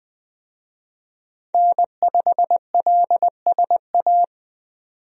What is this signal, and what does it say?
Morse code, heard as a keyed tone, about 20 wpm, 700 Hz. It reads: N5LSA
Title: USA Call Signs - 5 Characters - 20wpm